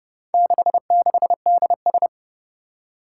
Morse code, heard as a keyed tone, about 29 wpm, 700 Hz. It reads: 66DS
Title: Four Challenging Letter-Number Combinations 30wpm